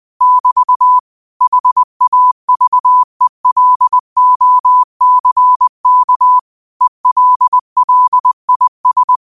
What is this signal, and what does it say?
Morse code, heard as a keyed tone, about 20 wpm, 1000 Hz. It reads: <BT> HAVELOCK ELLIS